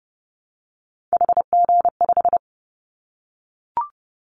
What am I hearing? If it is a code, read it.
Morse code, heard as a keyed tone, about 30 wpm, 700 Hz. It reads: HG5
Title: Callsign Prefixes for Top 50 DXCC - 30wpm